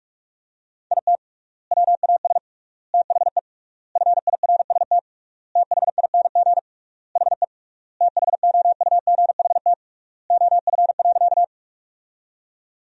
Morse code, heard as a keyed tone, about 41 wpm, 700 Hz. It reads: IT WAS THE FIRST THING HE THOUGHT OF.
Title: 3x Word-Spacing - Sentences from Top 300 Words - 45wpm